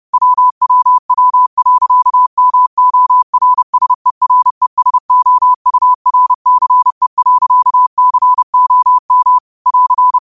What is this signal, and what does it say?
Morse code, heard as a keyed tone, about 30 wpm, 1000 Hz. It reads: WWW.MORSERESOURCE.COM <AR>
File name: qotdfiles / 30wpm